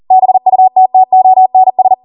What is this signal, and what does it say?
Morse code, heard as a keyed tone, about 40 wpm, 750 Hz. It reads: BUTTONS